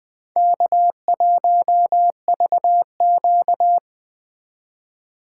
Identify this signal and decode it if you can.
Morse code, heard as a keyed tone, about 20 wpm, 700 Hz. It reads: K1VQ
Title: Speed Racing - USA 1x2 Call Signs 17wpm